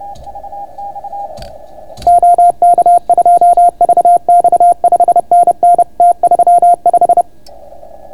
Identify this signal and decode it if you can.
Morse code, heard as a keyed tone, about 30 wpm, 690 Hz. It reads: OK2VX 5NN T35